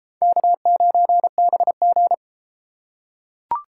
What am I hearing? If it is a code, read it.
Morse code, heard as a keyed tone, about 33 wpm, 700 Hz. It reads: K9BG